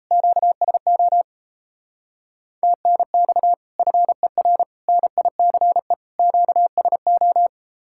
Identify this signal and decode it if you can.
Morse code, heard as a keyed tone, about 36 wpm, 700 Hz. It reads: QSO TNX FER NICE QSO